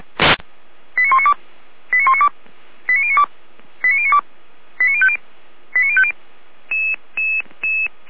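Five-tone selective call (ZVEI), heard as quick runs of five tones, tones of about 80 ms. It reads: 89192 89192 89092 89092 89050 89050
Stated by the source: ZVEI five-tone call received by SDR.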